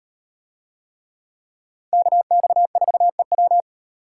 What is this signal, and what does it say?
Morse code, heard as a keyed tone, about 35 wpm, 700 Hz. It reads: KX4EW